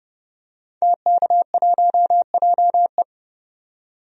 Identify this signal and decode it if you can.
Morse code, heard as a keyed tone, about 32 wpm, 700 Hz. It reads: TK1JE
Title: DXCC-100 Call Signs - 5 Characters - 30wpm